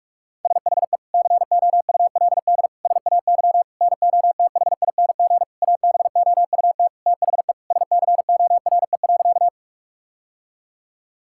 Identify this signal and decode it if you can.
Morse code, heard as a keyed tone, about 41 wpm, 700 Hz. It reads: SHE COULD SAY NOTHING ABOUT THE SCORE.
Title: Sentences from Top 1600 Words 45wpm